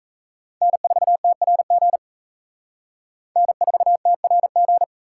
Morse code, heard as a keyed tone, about 35 wpm, 700 Hz. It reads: N4TRG N4TRG
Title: Speed Racing - USA 1x3 Call Signs 28wpm